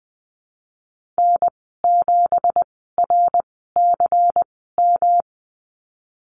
Morse code, heard as a keyed tone, about 20 wpm, spaced as for 15 wpm, 700 Hz. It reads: N7RCM
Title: 167 Lesson 27g - Number 7 - Call Signs 20wpm